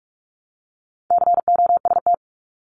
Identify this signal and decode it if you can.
Morse code, heard as a keyed tone, about 46 wpm, 700 Hz. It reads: COST
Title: Multi-letter Phonemes Repeated 3x 45wpm